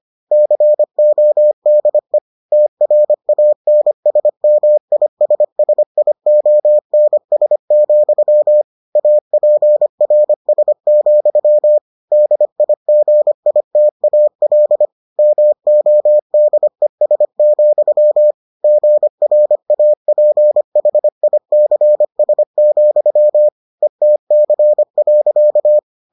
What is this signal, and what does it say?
Morse code, heard as a keyed tone, about 25 wpm, 600 Hz. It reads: CODE TRANSMISSIONS, APRS, DIGITAL MODES, GRAPHICS, ETC.